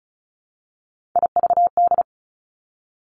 Morse code, heard as a keyed tone, about 36 wpm, 700 Hz. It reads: IVD
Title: Three Challenging Letter-Number Combinations 35wpm